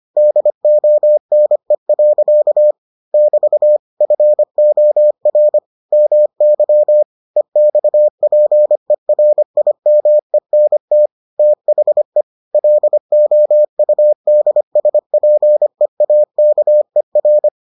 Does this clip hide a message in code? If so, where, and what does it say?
Yes: Morse code from 0.2 to 17.5 s, heard as a keyed tone, about 25 wpm, 600 Hz. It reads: DONE. <BT> FOR MY EXPERIMENT THE LOUDSPEAKER